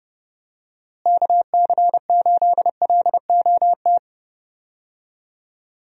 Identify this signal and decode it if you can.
Morse code, heard as a keyed tone, about 30 wpm, 700 Hz. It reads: KC8LOT
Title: USA Call Signs - 6 Characters - Repeated 2x - 30wpm